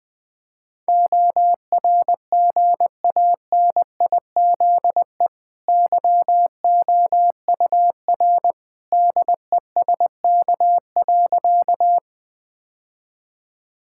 Morse code, heard as a keyed tone, about 20 wpm, 700 Hz. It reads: ORGANIZE YOUR DESK.
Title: Easy Sentences with Hints - 20wpm